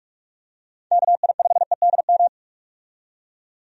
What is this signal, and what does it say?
Morse code, heard as a keyed tone, about 45 wpm, 700 Hz. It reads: KI5EDM
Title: USA Call Signs - 6 Characters - 45wpm